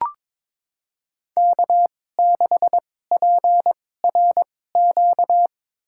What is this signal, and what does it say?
Morse code, heard as a keyed tone, about 21 wpm, spaced as for 16 wpm, 700 Hz. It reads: K6PRQ
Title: USA Call Signs - 5 Characters - 2x Character Spacing - 22wpm